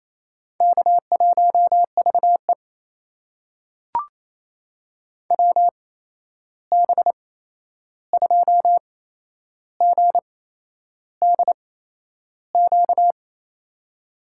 Morse code, heard as a keyed tone, about 28 wpm, spaced as for 11 wpm, 700 Hz. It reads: K1VE WB2GDQ
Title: USA Call Signs - 8x Character Spacing - 28wpm